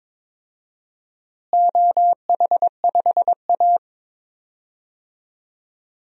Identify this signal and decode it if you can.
Morse code, heard as a keyed tone, about 21 wpm, 700 Hz. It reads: OH5A
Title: DXCC-100 Call Signs - 4 Characters - Repeated 3x - 22wpm